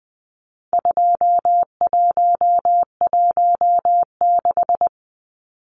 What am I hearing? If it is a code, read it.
Morse code, heard as a keyed tone, about 20 wpm, 700 Hz. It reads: 2116
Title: Numbers - 4-Digit 20wpm